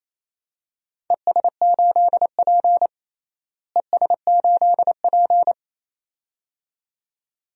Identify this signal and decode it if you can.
Morse code, heard as a keyed tone, about 27 wpm, 700 Hz. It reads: ES8P ES8P
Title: DXCC-100 Call Signs - 4 Characters - Repeated 2x - 28wpm